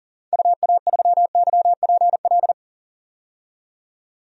Morse code, heard as a keyed tone, about 40 wpm, 700 Hz. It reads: UA3YPL
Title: DXCC-100 Call Signs - 6 Characters - 40wpm